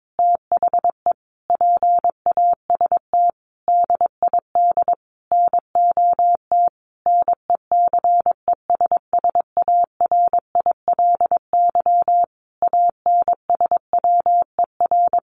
Morse code, heard as a keyed tone, about 22 wpm, 700 Hz. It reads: THE PAST DID NOT NECESSARILY ANSWER
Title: Sentences from Top 1400 Words 22wpm